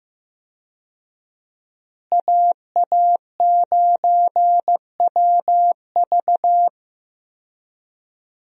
Morse code, heard as a keyed tone, about 15 wpm, 700 Hz. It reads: AA9WV
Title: USA Call Signs - 5 Characters - Repeated 3x - 15wpm